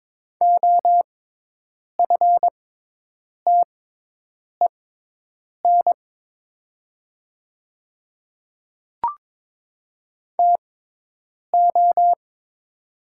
Morse code, heard as a keyed tone, about 22 wpm, spaced as for 8 wpm, 700 Hz. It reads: OFTEN TO
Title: 6x Character Spacing - Top 300 Words - 22wpm